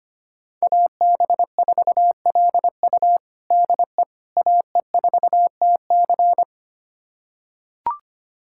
Morse code, heard as a keyed tone, about 25 wpm, 700 Hz. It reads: AB4LU DE AE4TC
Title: USA Callsign DE Callsign - 5 Characters - 25wpm